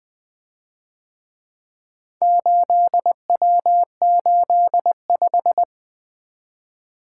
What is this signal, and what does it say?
Morse code, heard as a keyed tone, about 24 wpm, 700 Hz. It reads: 8W85
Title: Speed-Racing - Four Letter-Number Combinations - 20wpm